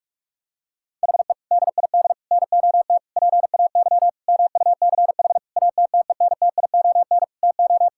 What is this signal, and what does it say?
Morse code, heard as a keyed tone, about 46 wpm, 700 Hz. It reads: HE DID NOT PAY MUCH ATTENTION TO
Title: Sentences from Top 1000 Words - Review Part 6 - 45wpm